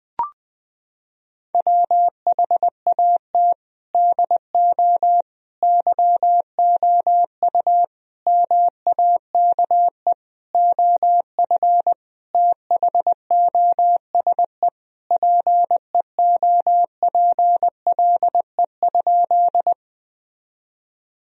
Morse code, heard as a keyed tone, about 20 wpm, 700 Hz. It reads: WHAT DO YOU MAKE OF THOSE PEOPLE?
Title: Sentences from Top 1000 Words - Review Part 1 - 20wpm